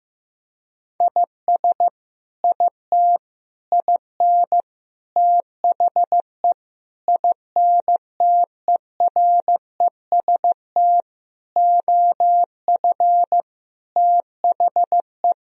Morse code, heard as a keyed tone, about 16 wpm, 700 Hz. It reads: IS IT IN THE INTEREST OF THE
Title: Sentences from Top 1000 Words - Review Part 4 - 15wpm